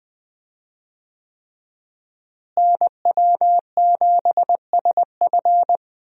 Morse code, heard as a keyed tone, about 20 wpm, 700 Hz. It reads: NW7SF
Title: USA Call Signs - 5 Characters - Repeated 3x - 20wpm